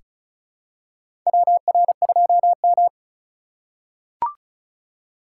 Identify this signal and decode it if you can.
Morse code, heard as a keyed tone, about 36 wpm, 700 Hz. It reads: WR2M
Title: USA Call Signs - 4 Characters - 2x Character Spacing - 35wpm